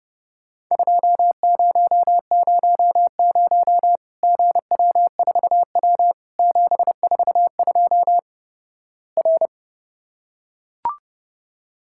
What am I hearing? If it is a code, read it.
Morse code, heard as a keyed tone, about 32 wpm, 700 Hz. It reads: 2000 GW4W 742 R
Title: Worked All Europe DX Contest - QTC Exchange - 30wpm